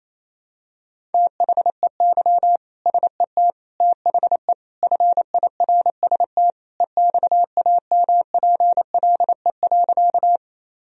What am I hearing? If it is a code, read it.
Morse code, heard as a keyed tone, about 28 wpm, 700 Hz. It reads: THEY SET THE FIRST EXAMPLE.
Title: Sentences from Top 1000 Words - Review Part 1 - 28wpm